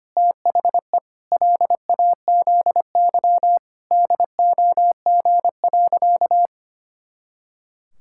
Morse code, heard as a keyed tone, about 25 wpm, 700 Hz. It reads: THE LAZY DOG.
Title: Pangram Challenge v2 - 25wpm